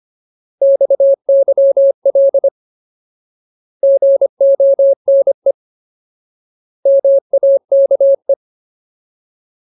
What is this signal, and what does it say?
Morse code, heard as a keyed tone, about 25 wpm, 550 Hz. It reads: XYL GONE MAKE
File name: Ultimate Word List / 25WPM Ultimate